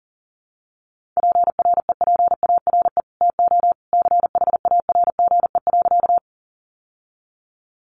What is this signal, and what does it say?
Morse code, heard as a keyed tone, about 40 wpm, 700 Hz. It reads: PREPARE TO CHARGE.
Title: Sentences from Top 900 Words 40wpm